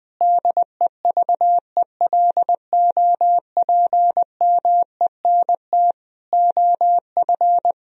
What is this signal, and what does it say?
Morse code, heard as a keyed tone, about 20 wpm, 700 Hz. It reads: DEVELOPMENT OF
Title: Sentences from Top 1000 Words - Review Part 4 - 20wpm